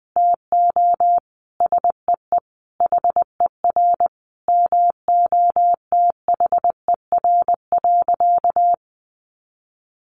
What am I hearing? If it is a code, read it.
Morse code, heard as a keyed tone, about 19 wpm, 700 Hz. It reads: TO SEE HER MOTHER.
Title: Sentences from Top 800 Words 20wpm